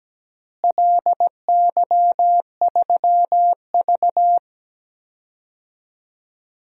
Morse code, heard as a keyed tone, about 17 wpm, 700 Hz. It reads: LY3V
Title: DXCC-100 Call Signs - 4 Characters - Repeated 2x - 17wpm